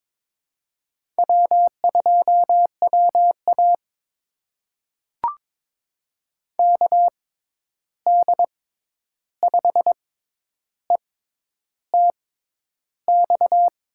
Morse code, heard as a keyed tone, about 22 wpm, spaced as for 10 wpm, 700 Hz. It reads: W2WA KD5ETX
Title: USA Call Signs - 6x Character Spacing - 22wpm